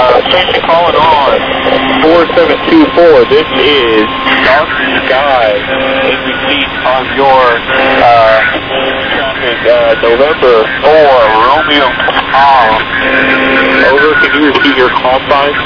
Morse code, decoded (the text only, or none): none